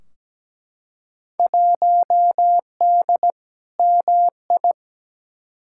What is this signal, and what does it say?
Morse code, heard as a keyed tone, about 17 wpm, 700 Hz. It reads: 1D MI
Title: Field Day Exchanges - 17wpm